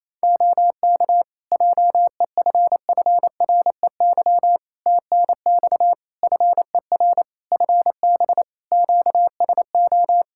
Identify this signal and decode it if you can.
Morse code, heard as a keyed tone, about 28 wpm, 700 Hz. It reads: OK JEFFREY TNX FER FB QSO
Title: Practice QSO - Exchange 4 - Ending - 28wpm